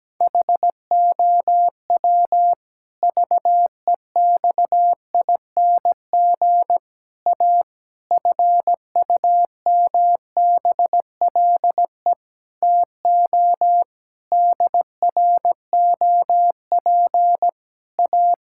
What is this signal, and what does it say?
Morse code, heard as a keyed tone, about 17 wpm, 700 Hz. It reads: HOW VEXING A FUMBLE TO DROP A
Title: Pangram Challenge v2 - 17wpm